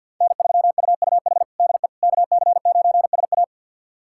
Morse code, heard as a keyed tone, about 52 wpm, 700 Hz. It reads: N3UUH DE KC9SA
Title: USA Callsign DE Callsign - 5 Characters - 50wpm